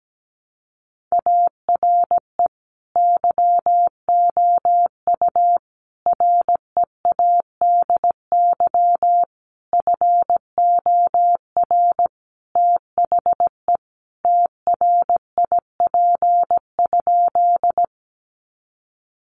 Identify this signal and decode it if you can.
Morse code, heard as a keyed tone, about 17 wpm, 700 Hz. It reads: ARE YOU READY FOR THE TRIP?
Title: Sentences from Top 1300 Words 17wpm